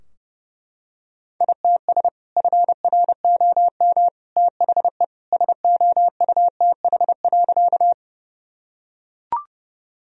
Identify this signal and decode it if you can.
Morse code, heard as a keyed tone, about 32 wpm, 700 Hz. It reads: ITS FROM THE SOUTH.
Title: Sentences from Top 1000 Words - Review Part 2 - 30wpm